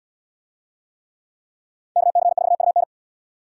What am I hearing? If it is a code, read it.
Morse code, heard as a keyed tone, about 77 wpm, 700 Hz. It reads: KC4UN